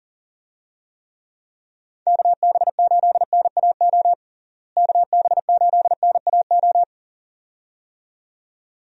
Morse code, heard as a keyed tone, about 40 wpm, 700 Hz. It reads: KB8NAO KB8NAO
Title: USA Call Signs - 6 Characters - Repeated 2x - 40wpm